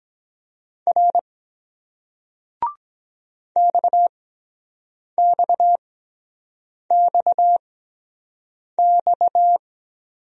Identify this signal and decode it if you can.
Morse code, heard as a keyed tone, about 23 wpm, spaced as for 7 wpm, 700 Hz. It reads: R XXXX